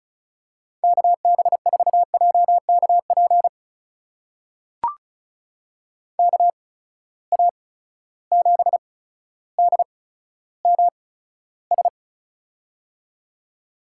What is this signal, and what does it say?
Morse code, heard as a keyed tone, about 36 wpm, spaced as for 15 wpm, 700 Hz. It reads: KB4JKP KA7DMS
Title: USA Call Signs - 6 Characters - 8x Character Spacing - 35wpm